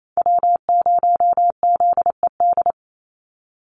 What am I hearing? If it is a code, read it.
Morse code, heard as a keyed tone, about 28 wpm, 700 Hz. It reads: W0ZED
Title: USA Call Signs 28wpm